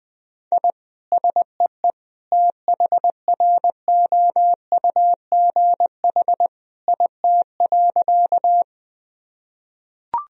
Morse code, heard as a keyed tone, about 20 wpm, 700 Hz. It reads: I SEE THROUGH IT.
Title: Sentences from Top 1000 Words - Review Part 1 - 20wpm